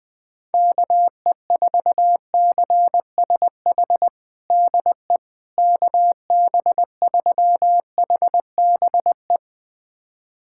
Morse code, heard as a keyed tone, about 20 wpm, 700 Hz. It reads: KE4CSH DE KB3HBE
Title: USA Callsign DE Callsign 20wpm